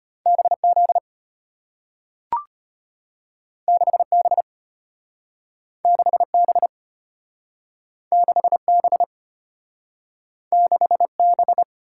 Morse code, heard as a keyed tone, about 34 wpm, 700 Hz. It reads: BZ 6B 6B 6B 6B